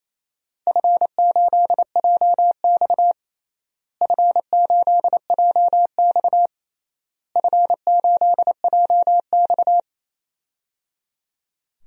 Morse code, heard as a keyed tone, about 28 wpm, 700 Hz. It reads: F8JX F8JX F8JX
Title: DXCC-100 Call Signs - 4 Characters - Repeated 3x - 28wpm